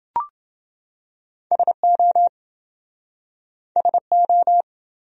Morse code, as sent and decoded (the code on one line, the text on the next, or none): ... --- / ... ---
SO SO